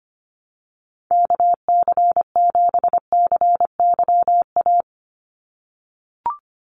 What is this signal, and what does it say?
Morse code, heard as a keyed tone, about 25 wpm, 700 Hz. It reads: KC7CYA
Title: USA Call Signs 25wpm